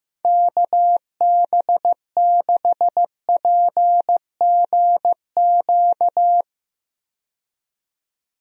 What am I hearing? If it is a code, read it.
Morse code, heard as a keyed tone, about 15 wpm, 700 Hz. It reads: KB6PGQ
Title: USA Call Signs - 6 Characters - 15wpm